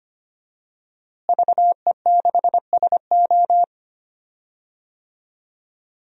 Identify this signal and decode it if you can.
Morse code, heard as a keyed tone, about 25 wpm, 700 Hz. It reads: VE6SO
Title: DXCC-100 Call Signs - 5 Characters - Repeated 3x - 25wpm